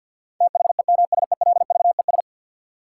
Morse code, heard as a keyed tone, about 52 wpm, 700 Hz. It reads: THEMSELVES
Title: Sentences from Top 800 Words - New Word Drill - 50wpm